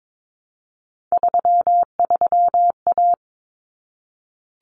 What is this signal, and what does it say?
Morse code, heard as a keyed tone, about 22 wpm, 700 Hz. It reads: 33A
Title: Common Cut Numbers - 3-Digit 22wpm